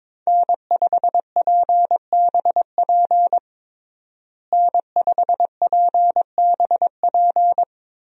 Morse code, heard as a keyed tone, about 22 wpm, 700 Hz. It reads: N5PBP N5PBP
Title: USA Call Signs - 5 Characters - Repeated 3x - 22wpm